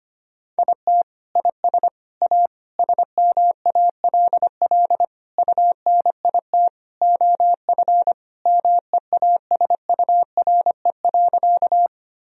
Morse code, heard as a keyed tone, about 25 wpm, 700 Hz. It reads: IT IS A SMALL UNIT OF MEASURE.
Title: Sentences from Top 1000 Words - Review Part 2 - 25wpm